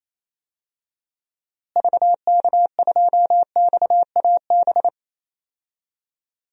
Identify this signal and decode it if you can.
Morse code, heard as a keyed tone, about 27 wpm, 700 Hz. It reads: VK2XAB
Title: DXCC-100 Call Signs - 6 Characters - Repeated 2x - 28wpm